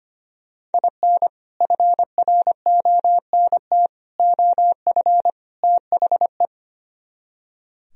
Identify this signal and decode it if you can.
Morse code, heard as a keyed tone, about 32 wpm, 700 Hz. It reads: IN FRONT OF THE